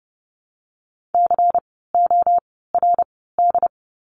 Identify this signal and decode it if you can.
Morse code, heard as a keyed tone, about 32 wpm, spaced as for 17 wpm, 700 Hz. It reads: CORD
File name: Lesson 16f - Letter c - Words 30wpm